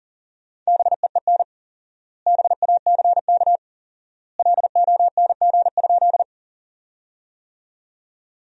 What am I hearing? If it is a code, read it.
Morse code, heard as a keyed tone, about 40 wpm, 700 Hz. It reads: BEEN BACK LONG?
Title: 4x Word-Spacing - Sentences from Top 200 Words - 40wpm